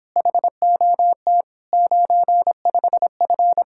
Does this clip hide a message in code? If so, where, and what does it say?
Yes: Morse code from 0.2 to 3.6 s, heard as a keyed tone, about 26 wpm, 700 Hz. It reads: HOT 95F